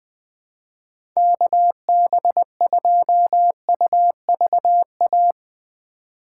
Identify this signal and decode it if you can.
Morse code, heard as a keyed tone, about 20 wpm, 700 Hz. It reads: KB2UVA
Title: USA Call Signs - 2x Character Spacing - 20wpm